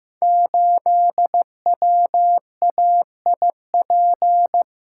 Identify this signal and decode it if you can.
Morse code, heard as a keyed tone, about 15 wpm, 700 Hz. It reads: 8WAIP
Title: Code Groups - 5 Letter:Numbers - 15wpm